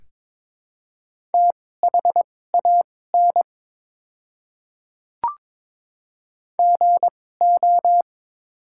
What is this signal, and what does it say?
Morse code, heard as a keyed tone, about 21 wpm, spaced as for 16 wpm, 700 Hz. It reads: THAN GO
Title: 2x Character Spacing - Top 200 Words - 22wpm